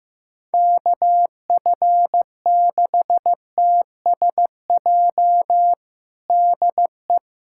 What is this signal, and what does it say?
Morse code, heard as a keyed tone, about 15 wpm, 700 Hz. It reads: KF6TSJ DE
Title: USA Callsign DE Callsign - 6 Characters - 15wpm